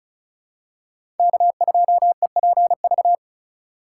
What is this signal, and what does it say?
Morse code, heard as a keyed tone, about 36 wpm, 700 Hz. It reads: K2EPV
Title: USA Call Signs - 5 Characters - 2x Character Spacing - 35wpm